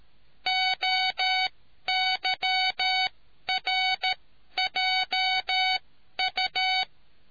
Morse code, heard as a keyed tone, about 13 wpm, spaced as for 11 wpm, 750 Hz. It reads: OYRJU